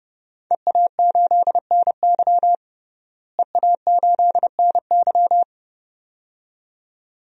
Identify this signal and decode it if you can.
Morse code, heard as a keyed tone, about 30 wpm, 700 Hz. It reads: EA8NY EA8NY
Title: DXCC-100 Call Signs - 5 Characters - Repeated 2x - 30wpm